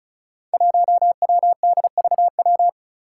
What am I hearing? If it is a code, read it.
Morse code, heard as a keyed tone, about 34 wpm, 700 Hz. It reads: 1WDVW
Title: Five Challenging Letter-Number Combinations 35wpm